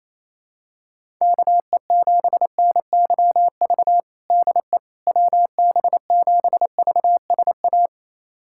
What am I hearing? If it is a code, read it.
Morse code, heard as a keyed tone, about 27 wpm, 700 Hz. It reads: KE7NYV DE WB7VSA
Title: USA Callsign DE Callsign 28wpm